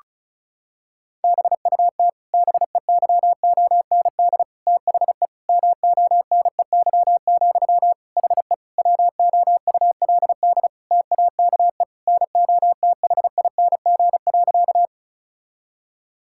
Morse code, heard as a keyed tone, about 36 wpm, 700 Hz. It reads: BUT BEYOND THE MONEY, HE WOULD TAKE NOTHING.
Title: Sentences from Top 1000 Words - Review Part 6 - 35wpm